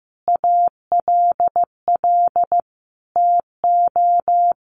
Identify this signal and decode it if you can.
Morse code, heard as a keyed tone, about 15 wpm, 700 Hz. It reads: ALL TO
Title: Sentences from Top 1200 Words 15wpm